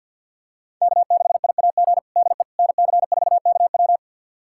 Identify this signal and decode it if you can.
Morse code, heard as a keyed tone, about 50 wpm, 690 Hz. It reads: K6IAG DE NC4KW
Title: USA Callsign DE Callsign - 5 Characters - 50wpm